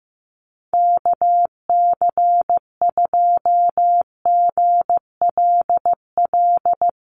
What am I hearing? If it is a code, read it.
Morse code, heard as a keyed tone, about 15 wpm, 700 Hz. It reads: KC2GLL
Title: USA Call Signs Repeated 3x 15wpm